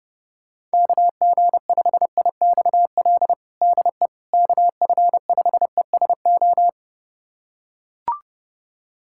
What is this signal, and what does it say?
Morse code, heard as a keyed tone, about 30 wpm, 690 Hz. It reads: KG5IXL DE KF5ESO
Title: USA Callsign DE Callsign - 6 Characters - 30wpm